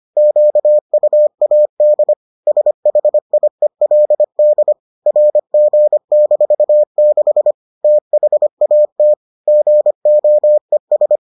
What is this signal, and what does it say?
Morse code, heard as a keyed tone, about 25 wpm, 600 Hz. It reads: QUAD SHIELD RG-6 THAT GOES